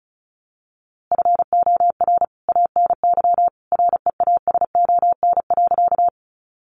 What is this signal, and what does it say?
Morse code, heard as a keyed tone, about 34 wpm, 700 Hz. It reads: FOR ANY REASON.
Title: A Sound Of Thunder 35wpm